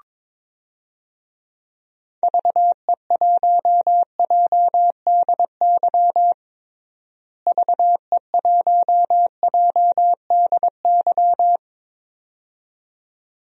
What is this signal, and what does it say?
Morse code, heard as a keyed tone, about 22 wpm, 700 Hz. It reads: VE1JDY VE1JDY
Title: DXCC-100 Call Signs - 6 Characters - Repeated 2x - 22wpm